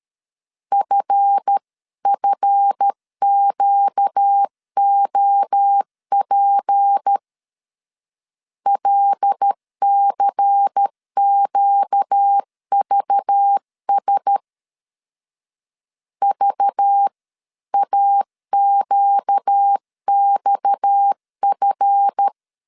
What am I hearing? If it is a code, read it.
Morse code, heard as a keyed tone, about 13 wpm, 800 Hz. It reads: FFQOP LCQVS VAQXF
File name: LEZIONI / 7 GF9Z / 7 GF 12wpm 2122